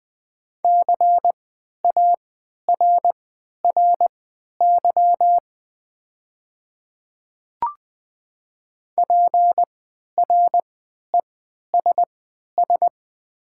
Morse code, heard as a keyed tone, about 20 wpm, spaced as for 11 wpm, 700 Hz. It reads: CARRY PRESS
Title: 3x Character Spacing - Top 300 Words - 20wpm